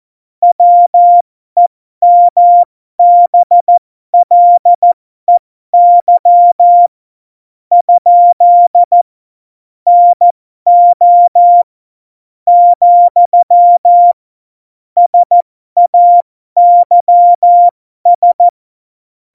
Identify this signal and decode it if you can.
Morse code, heard as a keyed tone, about 14 wpm, spaced as for 12 wpm, 700 Hz. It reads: WEMBLEY ? NO , SAYS
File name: Morsekurs / Witze